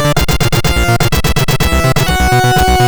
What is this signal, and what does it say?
There is no code.